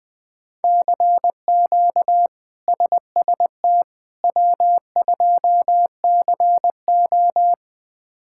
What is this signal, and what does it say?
Morse code, heard as a keyed tone, about 20 wpm, 700 Hz. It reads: CQ SST W2CO
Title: K1USN SST QSO Practice - 20wpm